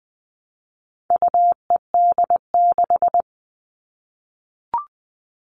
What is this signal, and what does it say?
Morse code, heard as a keyed tone, about 20 wpm, 700 Hz. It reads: UED6
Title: All Cut Numbers - 4-Digit 20wpm